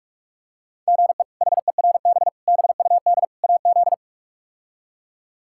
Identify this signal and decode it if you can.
Morse code, heard as a keyed tone, about 46 wpm, 700 Hz. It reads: GE HERB BUD AZ
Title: K1USN SST QSO Practice - 45wpm